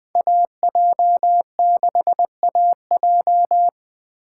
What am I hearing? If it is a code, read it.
Morse code, heard as a keyed tone, about 20 wpm, 700 Hz. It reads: AJ6AJ
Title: Speed Racing - USA Call Signs 17wpm